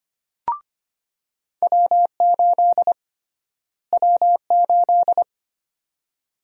Morse code, heard as a keyed tone, about 25 wpm, 700 Hz. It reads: W8 W8
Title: USA Call Sign Prefixes - Repeated 2x - 25wpm